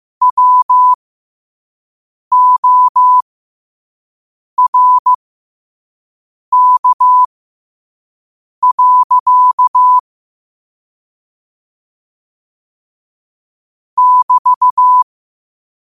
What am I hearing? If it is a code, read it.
Morse code, heard as a keyed tone, about 15 wpm, spaced as for 5 wpm, 1000 Hz. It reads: WORK. <BT>
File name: qotdfiles / 5wpm